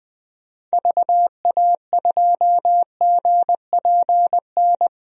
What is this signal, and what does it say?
Morse code, heard as a keyed tone, about 20 wpm, 700 Hz. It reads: VA2GPN
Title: DXCC-100 Call Signs - 6 Characters - 20wpm